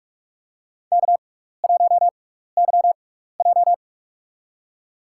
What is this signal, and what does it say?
Morse code, heard as a keyed tone, about 46 wpm, spaced as for 15 wpm, 700 Hz. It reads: K1YJ
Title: USA Call Signs - 4 Characters - 6x Character Spacing - 45wpm